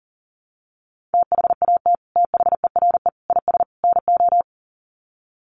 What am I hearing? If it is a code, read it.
Morse code, heard as a keyed tone, about 40 wpm, 700 Hz. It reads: THAT THERE IS NO
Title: Sets of 4 Words 40wpm